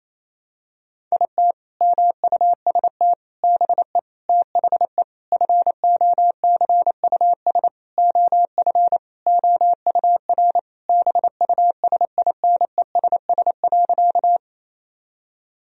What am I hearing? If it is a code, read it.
Morse code, heard as a keyed tone, about 28 wpm, 700 Hz. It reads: IT MUST BE THE FOCUS OF OUR BUSINESS.
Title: Sentences from Top 1000 Words - Review Part 4 - 28wpm